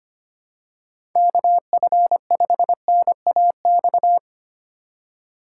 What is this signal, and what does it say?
Morse code, heard as a keyed tone, about 25 wpm, 700 Hz. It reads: KF5NAX
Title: USA Call Signs - 6 Characters - 2x Character Spacing - 25wpm